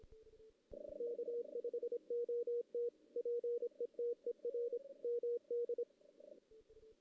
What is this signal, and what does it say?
Morse code, heard as a keyed tone, about 24 wpm, 480 Hz. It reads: K5OT PETER MD
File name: 2016 JAN CW NAQP / 2016 Jan naqp k5ot - 0-2